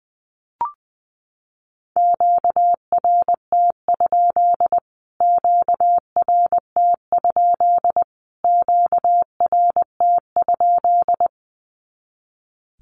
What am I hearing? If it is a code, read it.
Morse code, heard as a keyed tone, about 20 wpm, 700 Hz. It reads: QRT? QRT? QRT?
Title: Q-Codes Repeated 3x 20wpm